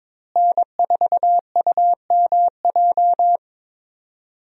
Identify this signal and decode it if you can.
Morse code, heard as a keyed tone, about 22 wpm, 700 Hz. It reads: N4UMJ
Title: USA Call Signs - 5 Characters - Repeated 2x - 22wpm